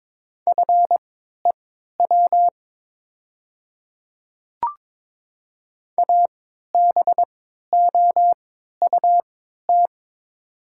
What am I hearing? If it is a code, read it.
Morse code, heard as a keyed tone, about 22 wpm, spaced as for 12 wpm, 700 Hz. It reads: FEW ABOUT
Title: 3x Character Spacing - Top 300 Words - 22wpm